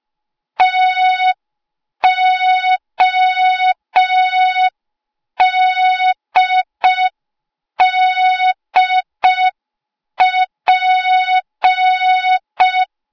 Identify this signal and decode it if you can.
Morse code, heard as a keyed tone, about 5 wpm, 3700 Hz. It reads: TODDP